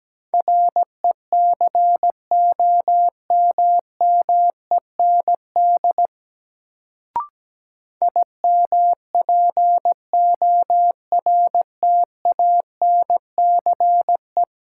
Morse code, heard as a keyed tone, about 17 wpm, 700 Hz. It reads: RECOMMEND IMPORTANCE
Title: Sentences from Top 1200 Words - New Word Drill Alt - 17wpm